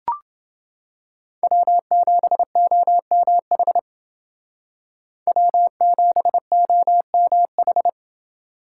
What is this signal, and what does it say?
Morse code, heard as a keyed tone, about 28 wpm, 700 Hz. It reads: W7OMH W7OMH